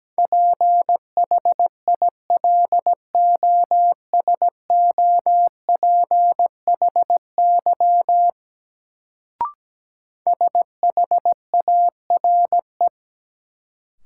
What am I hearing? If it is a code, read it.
Morse code, heard as a keyed tone, about 17 wpm, 700 Hz. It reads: PHILOSOPHY SHARE
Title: Sentences from Top 900 Words - New Word Drill - 17wpm